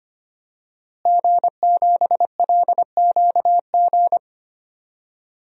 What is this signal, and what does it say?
Morse code, heard as a keyed tone, about 25 wpm, 700 Hz. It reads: G7LQG
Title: DXCC-100 Call Signs - 5 Characters - Repeated 2x - 25wpm